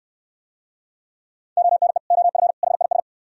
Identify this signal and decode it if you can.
Morse code, heard as a keyed tone, about 64 wpm, 700 Hz. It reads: ONE OF HIS